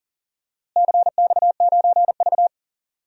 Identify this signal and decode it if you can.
Morse code, heard as a keyed tone, about 40 wpm, 700 Hz. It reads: CX9V
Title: DXCC-100 Call Signs - 4 Characters - Repeated 2x - 40wpm